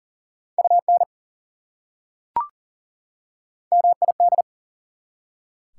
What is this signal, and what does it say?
Morse code, heard as a keyed tone, about 40 wpm, 700 Hz. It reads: UN MID
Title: Prefixes and Suffixes 40wpm